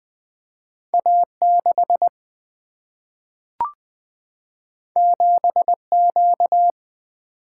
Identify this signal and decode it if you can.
Morse code, heard as a keyed tone, about 20 wpm, 700 Hz. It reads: A6 7Q
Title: Single-Letter Single-Number 20wpm